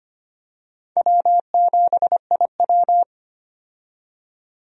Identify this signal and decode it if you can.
Morse code, heard as a keyed tone, about 25 wpm, 700 Hz. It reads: W7IW
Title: USA Call Signs - 4 Characters - 25wpm